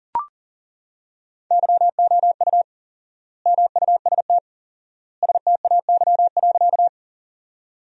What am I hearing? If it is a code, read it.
Morse code, heard as a keyed tone, about 40 wpm, 700 Hz. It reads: YOU MUST STAY.
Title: 4x Word-Spacing - Sentences from Top 500 Words - 40wpm